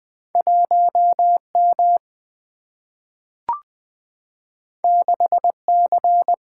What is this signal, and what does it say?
Morse code, heard as a keyed tone, about 20 wpm, 700 Hz. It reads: 1M 6C
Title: Single-Letter Single-Number 20wpm